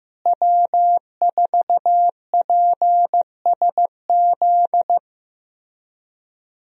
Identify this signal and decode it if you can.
Morse code, heard as a keyed tone, about 15 wpm, 700 Hz. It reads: W4PSZ
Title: USA Call Signs - 5 Characters - Repeated 3x - 15wpm